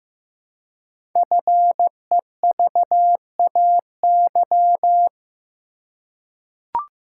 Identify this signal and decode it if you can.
Morse code, heard as a keyed tone, about 15 wpm, 700 Hz. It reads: FEVAY
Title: Code Groups - 5 Letter:Numbers - 15wpm